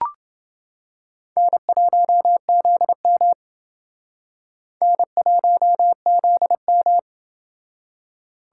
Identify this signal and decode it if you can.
Morse code, heard as a keyed tone, about 29 wpm, 700 Hz. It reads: N1ZM N1ZM